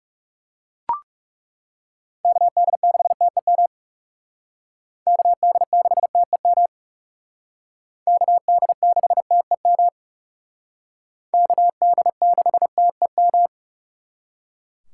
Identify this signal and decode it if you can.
Morse code, heard as a keyed tone, about 38 wpm, 700 Hz. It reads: KD6TEM KD6TEM KD6TEM KD6TEM
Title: Speed Racing - USA 2x3 Call Signs 30wpm